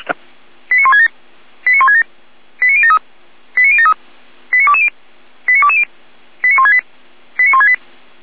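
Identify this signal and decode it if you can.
Five-tone selective call (ZVEI), heard as quick runs of five tones, tones of about 70 ms. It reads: 89167 89167 89073 89073 89220 89220 89168 89168
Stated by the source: ZVEI five-tone call received by SDR.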